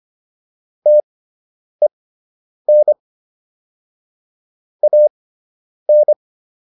Morse code, heard as a keyed tone, about 25 wpm, spaced as for 9 wpm, 600 Hz. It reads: TEN AN